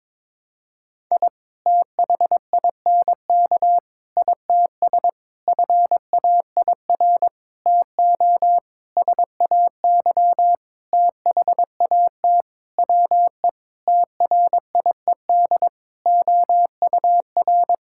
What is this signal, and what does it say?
Morse code, heard as a keyed tone, about 22 wpm, 700 Hz. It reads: I THINK ITS FAIR TO SAY THAT WE TRIED OUR
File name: Sentences with 7-Word Phrases - 22wpm